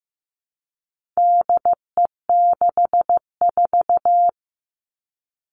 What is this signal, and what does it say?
Morse code, heard as a keyed tone, about 15 wpm, 700 Hz. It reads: DE64
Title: All Cut Numbers - 4-Digit 15wpm